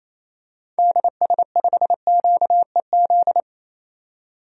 Morse code, heard as a keyed tone, about 28 wpm, 700 Hz. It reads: DS5QEZ
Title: DXCC-100 Call Signs - 6 Characters - 28wpm